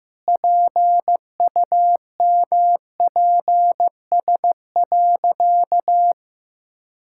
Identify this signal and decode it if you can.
Morse code, heard as a keyed tone, about 15 wpm, 700 Hz. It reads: PUMPS.
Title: Pangram Challenge v2 - 15wpm